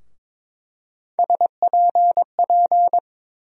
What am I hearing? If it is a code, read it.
Morse code, heard as a keyed tone, about 22 wpm, 700 Hz. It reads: SPP
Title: USA Call Sign Postfixes - 3 Letters 22wpm